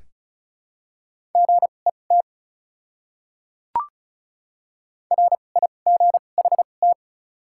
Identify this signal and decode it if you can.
Morse code, heard as a keyed tone, about 34 wpm, spaced as for 25 wpm, 700 Hz. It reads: GET RIGHT
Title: 2x Character Spacing - Top 300 Words - 35wpm